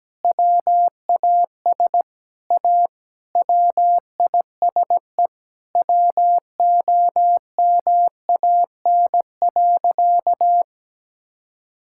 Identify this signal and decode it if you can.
Morse code, heard as a keyed tone, about 17 wpm, 700 Hz. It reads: WAS A WISE WOMAN.
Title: Sentences from Top 1600 Words 17wpm